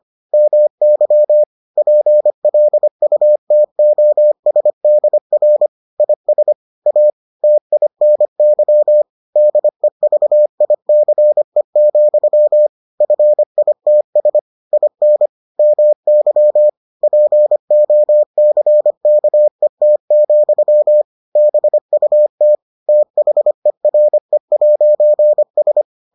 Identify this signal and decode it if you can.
Morse code, heard as a keyed tone, about 25 wpm, 600 Hz. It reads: MY PLUTOSDR IS A TINY DEVICE, FITS IN MY POCKET, BUT THERE'S